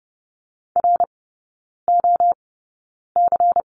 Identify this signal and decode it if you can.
Morse code, heard as a keyed tone, about 30 wpm, spaced as for 9 wpm, 700 Hz. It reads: ROC
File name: Lesson 16e - Letter c - Three Characters 30wpm